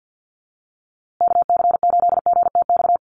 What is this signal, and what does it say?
Morse code, heard as a keyed tone, about 50 wpm, 700 Hz. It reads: KC8GTX
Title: USA Call Signs Repeated 2x 50wpm